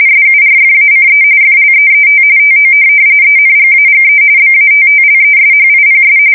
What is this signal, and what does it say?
RTTY, about 45.45 baud. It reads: SURFACE FLEET TO ENGAGE BATTLESHIPS
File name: fsk170 / USS-New-Jersey